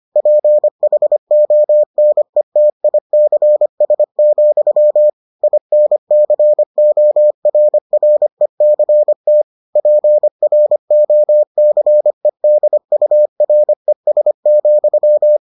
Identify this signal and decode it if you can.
Morse code, heard as a keyed tone, about 25 wpm, 600 Hz. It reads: PHONETICS, INCORRECT PROCEDURES,